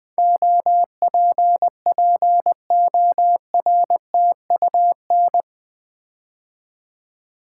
Morse code, heard as a keyed tone, about 20 wpm, 700 Hz. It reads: OPPORTUN
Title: Build A Word - 4 Letters At A Time - 11-Letter Words - 20wpm